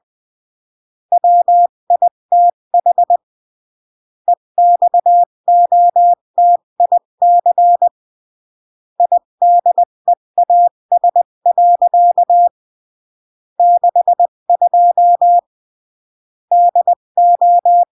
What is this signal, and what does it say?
Morse code, heard as a keyed tone, about 20 wpm, 700 Hz. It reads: WITH EXOTIC IDEAS. 62 DO
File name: CW / jodorowsky / 82 commandments 20-15